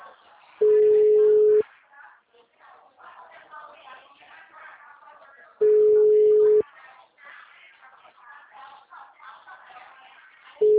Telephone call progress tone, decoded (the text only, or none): ringback tone